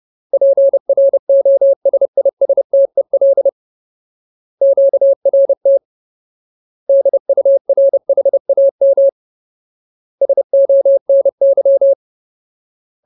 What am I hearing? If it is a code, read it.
Morse code, heard as a keyed tone, about 30 wpm, 550 Hz. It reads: PROSISTEL QRT DURHAM SONY
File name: Ultimate Word List / 30WPM Ultimate